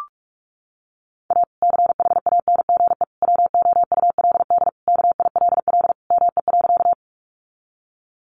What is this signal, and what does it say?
Morse code, heard as a keyed tone, about 44 wpm, 700 Hz. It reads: A CHANGE WOULD KILL ME.
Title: Sentences from Top 1400 Words 45wpm